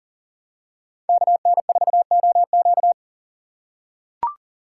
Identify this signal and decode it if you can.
Morse code, heard as a keyed tone, about 40 wpm, 700 Hz. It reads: KN4OQ
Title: USA Call Signs - 5 Characters - 3x Character Spacing - 40wpm